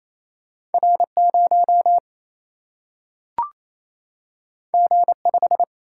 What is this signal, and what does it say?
Morse code, heard as a keyed tone, about 28 wpm, 700 Hz. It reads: R0 G5
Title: Single-Letter Single-Number 28wpm